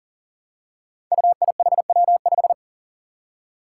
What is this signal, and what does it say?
Morse code, heard as a keyed tone, about 39 wpm, 700 Hz. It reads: UIHW5
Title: Five Challenging Letter-Number Combinations 40wpm